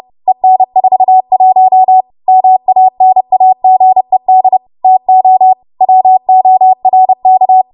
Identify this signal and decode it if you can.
Morse code, heard as a keyed tone, about 30 wpm, 750 Hz. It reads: EN41 MANAGED TO WORK
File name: Morse / 30WPM